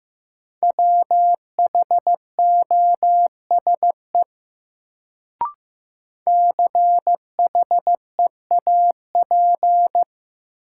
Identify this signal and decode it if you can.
Morse code, heard as a keyed tone, about 15 wpm, 700 Hz. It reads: WHOSE CHEAP
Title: Sentences from Top 1200 Words - New Word Drill Alt - 15wpm